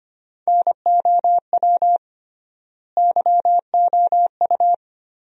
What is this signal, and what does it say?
Morse code, heard as a keyed tone, about 25 wpm, 700 Hz. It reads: NOW YOU
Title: 3x Word-Spacing - Sentences from Top 100 Words - 25wpm